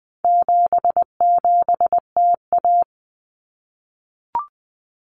Morse code, heard as a keyed tone, about 20 wpm, 700 Hz. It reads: 77TA
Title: Common Cut Numbers - 4-Digit 20wpm